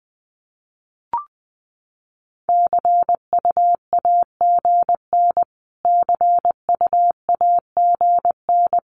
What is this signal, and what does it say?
Morse code, heard as a keyed tone, about 20 wpm, 700 Hz. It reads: CUAGN CUAGN
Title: Abbreviations Repeated 3x - 20wpm